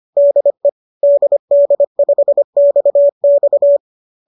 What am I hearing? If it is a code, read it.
Morse code, heard as a keyed tone, about 25 wpm, 590 Hz. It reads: DE DD5XX
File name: qso218 25